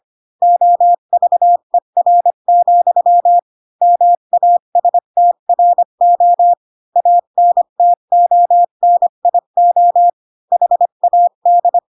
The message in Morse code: OVER, MASTRO ANTONIO HAD

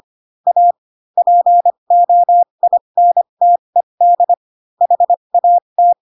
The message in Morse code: A POINTED HAT

